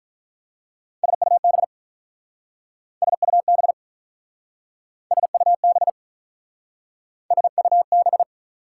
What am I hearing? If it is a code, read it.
Morse code, heard as a keyed tone, about 45 wpm, 690 Hz. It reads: SUB SUB SUB SUB